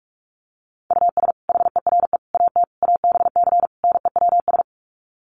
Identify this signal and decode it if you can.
Morse code, heard as a keyed tone, about 44 wpm, 700 Hz. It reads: US HERE AT ABC NEWS
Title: Sets of 5 Words 45wpm